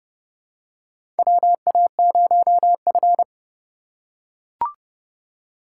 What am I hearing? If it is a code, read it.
Morse code, heard as a keyed tone, about 31 wpm, 700 Hz. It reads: WA0F
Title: USA Call Signs - 3x Character Spacing - 30wpm